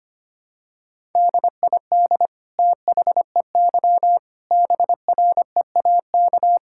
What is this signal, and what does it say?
Morse code, heard as a keyed tone, about 25 wpm, 700 Hz. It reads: DID THEY BREAK
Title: Sentences from Top 1000 Words - Review Part 6 - 25wpm